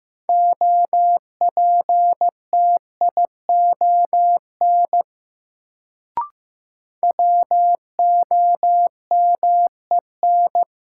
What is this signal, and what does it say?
Morse code, heard as a keyed tone, about 15 wpm, 700 Hz. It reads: OPTION WOMEN
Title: Sentences from Top 1300 Words - New Word Drill Alt - 15wpm